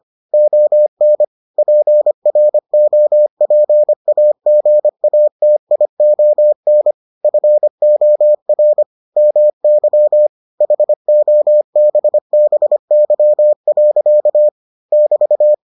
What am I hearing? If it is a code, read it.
Morse code, heard as a keyed tone, about 25 wpm, 600 Hz. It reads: ON PROPAGATION FOR MY HOBBY. <BT>